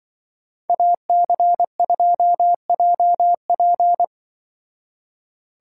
Morse code, heard as a keyed tone, about 24 wpm, 700 Hz. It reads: AC2JP
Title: Speed Racing - USA 2x2 Call Signs 20wpm